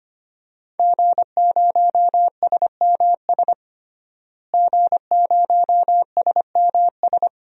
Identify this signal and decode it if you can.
Morse code, heard as a keyed tone, about 25 wpm, 700 Hz. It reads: G0SMS G0SMS
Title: DXCC-100 Call Signs - 5 Characters - Repeated 2x - 25wpm